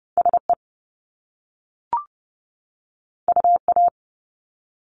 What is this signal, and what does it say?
Morse code, heard as a keyed tone, about 30 wpm, 700 Hz. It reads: SE UA
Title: Top Two Letter Combinations in Words 30wpm